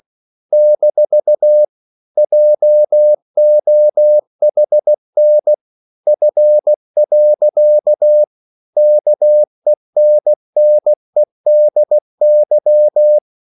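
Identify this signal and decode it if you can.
Morse code, heard as a keyed tone, about 16 wpm, 600 Hz. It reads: - JOHN F. KENNEDY
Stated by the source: English quotations sent in CW as practice at 16 words per minute.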